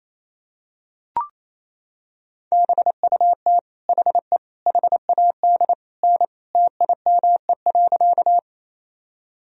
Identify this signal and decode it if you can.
Morse code, heard as a keyed tone, about 28 wpm, 700 Hz. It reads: BUT HE HAD N TIME.
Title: Sentences from Top 100 Words and Abbreviations 28wpm